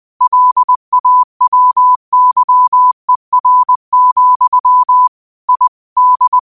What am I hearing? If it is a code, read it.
Morse code, heard as a keyed tone, about 20 wpm, 1000 Hz. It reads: LAWYER, I D
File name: qotdfiles / 20wpm